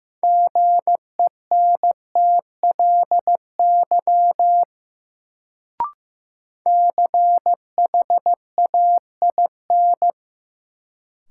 Morse code, heard as a keyed tone, about 15 wpm, 700 Hz. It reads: GENTLY CHAIN
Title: Sentences from Top 1500 Words - New Word Drill - 15wpm